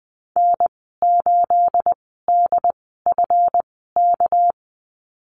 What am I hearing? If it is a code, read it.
Morse code, heard as a keyed tone, about 20 wpm, 700 Hz. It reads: N 8 D F K
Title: 238 Lesson 38g - Letter q - Call Signs 20wpm